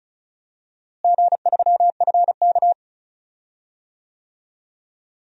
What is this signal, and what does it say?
Morse code, heard as a keyed tone, about 36 wpm, 700 Hz. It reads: G3FK
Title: DXCC-100 Call Signs - 4 Characters - Repeated 2x - 35wpm